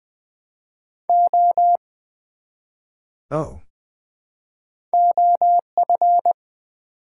Morse code, heard as a keyed tone, about 20 wpm, 700 Hz. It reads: O OF
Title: Build A Word - 1 Letter At A Time - 5-Letter Words - 20wpm